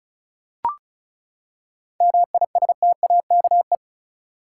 Morse code, heard as a keyed tone, about 34 wpm, 700 Hz. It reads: MISTAKE
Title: Seven Letter Words - 35wpm